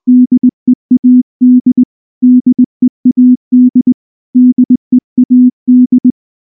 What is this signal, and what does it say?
Morse code, heard as a keyed tone, about 20 wpm, 250 Hz. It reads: DEAD DEAD DEAD
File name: morse-training-20WPM